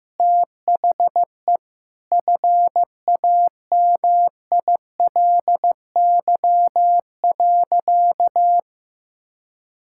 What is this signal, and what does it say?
Morse code, heard as a keyed tone, about 15 wpm, 700 Hz. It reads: THE FAMILY.
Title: Sentences from Top 1000 Words - Review Part 5 - 15wpm